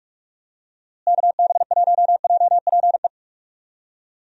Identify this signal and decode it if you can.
Morse code, heard as a keyed tone, about 43 wpm, 700 Hz. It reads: KB1JPE
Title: USA Call Signs - 6 Characters - Repeated 3x - 45wpm